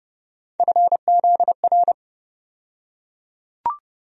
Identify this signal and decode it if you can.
Morse code, heard as a keyed tone, about 30 wpm, 700 Hz. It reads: FZR
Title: USA Call Sign Postfixes - 3 Letters 30wpm